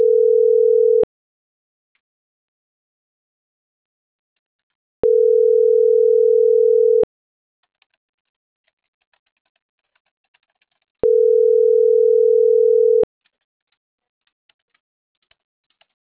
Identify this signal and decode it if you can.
Ringback tone (North America), 440 + 480 Hz.